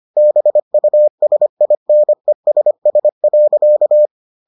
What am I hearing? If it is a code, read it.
Morse code, heard as a keyed tone, about 25 wpm, 600 Hz. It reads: BUSINESS.